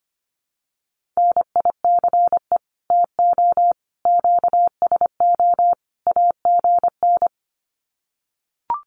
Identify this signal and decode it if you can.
Morse code, heard as a keyed tone, about 24 wpm, 700 Hz. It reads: NICE TO QSO AGN
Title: Common QSO Phrases 25wpm